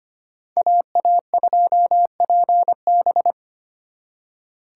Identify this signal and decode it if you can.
Morse code, heard as a keyed tone, about 31 wpm, 700 Hz. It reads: AA2PB
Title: Speed Racing - USA 2x2 Call Signs 25wpm